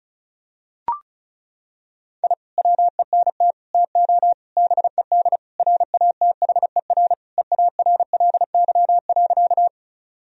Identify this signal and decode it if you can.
Morse code, heard as a keyed tone, about 34 wpm, 700 Hz. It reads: I WENT TO BED RATHER EARLY.
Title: Sentences from Top 1000 Words - Review Part 4 - 35wpm